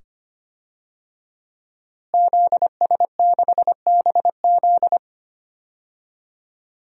Morse code, heard as a keyed tone, about 25 wpm, 700 Hz. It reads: ZS6BZ
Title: DXCC-100 Call Signs - 5 Characters - Repeated 2x - 25wpm